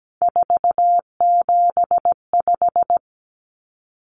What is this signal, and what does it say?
Morse code, heard as a keyed tone, about 17 wpm, 700 Hz. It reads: 475
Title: Common Cut Numbers - 3-Digit 17wpm